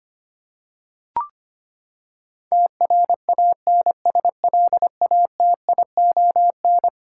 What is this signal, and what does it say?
Morse code, heard as a keyed tone, about 25 wpm, 700 Hz. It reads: TRANSLATION
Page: morse-code-ninja-practice-25wpm